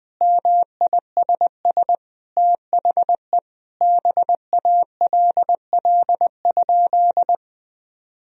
Morse code, heard as a keyed tone, about 20 wpm, 700 Hz. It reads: MISS THE BALL?
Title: Sentences from Top 1000 Words - Review Part 3 - 20wpm